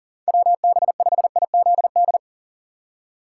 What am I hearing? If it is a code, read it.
Morse code, heard as a keyed tone, about 37 wpm, 700 Hz. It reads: WB5IZD
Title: USA Call Signs - 6x Character Spacing - 40wpm